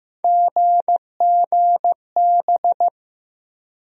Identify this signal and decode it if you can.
Morse code, heard as a keyed tone, about 15 wpm, 700 Hz. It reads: GGB
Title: USA Call Sign Postfixes - 3 Letters 15wpm